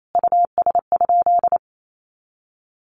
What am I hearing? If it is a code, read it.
Morse code, heard as a keyed tone, about 28 wpm, 700 Hz. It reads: US?
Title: Sentences from Top 600 Words 28wpm